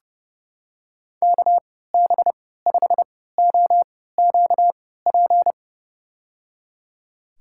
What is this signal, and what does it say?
Morse code, heard as a keyed tone, about 30 wpm, spaced as for 17 wpm, 700 Hz. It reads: KB5OQP
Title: USA Call Signs - 6 Characters - 3x Character Spacing - 30wpm